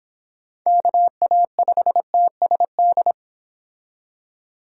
Morse code, heard as a keyed tone, about 31 wpm, 700 Hz. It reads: KA5TSD